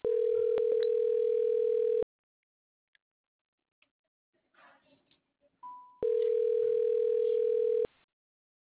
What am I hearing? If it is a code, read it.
Ringback tone (North America), 440 + 480 Hz.